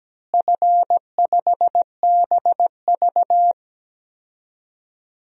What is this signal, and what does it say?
Morse code, heard as a keyed tone, about 17 wpm, 700 Hz. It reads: F5BV
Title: DXCC-100 Call Signs - 4 Characters - Repeated 2x - 17wpm